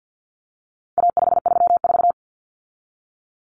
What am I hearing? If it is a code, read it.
Morse code, heard as a keyed tone, about 50 wpm, 700 Hz. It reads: A534
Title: Common Cut Numbers - 4-Digit 50wpm